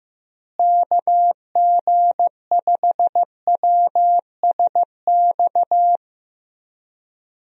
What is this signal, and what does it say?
Morse code, heard as a keyed tone, about 16 wpm, 700 Hz. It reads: KG5WSX
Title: USA Call Signs - 6 Characters - Repeated 3x - 15wpm